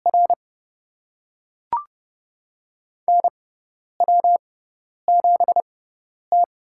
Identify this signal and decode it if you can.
Morse code, heard as a keyed tone, about 30 wpm, spaced as for 10 wpm, 700 Hz. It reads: R NW7T
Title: USA Call Signs - 4 Characters - 6x Character Spacing - 30wpm